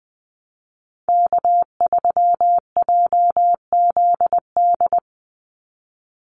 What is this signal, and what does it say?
Morse code, heard as a keyed tone, about 20 wpm, 700 Hz. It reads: K3JZD
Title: NA SOTA Chaser Callsigns 20wpm